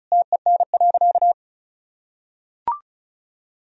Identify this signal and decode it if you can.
Morse code, heard as a keyed tone, about 36 wpm, 700 Hz. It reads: TEN.
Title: Sentences from Top 1000 Words - Review Part 2 - 35wpm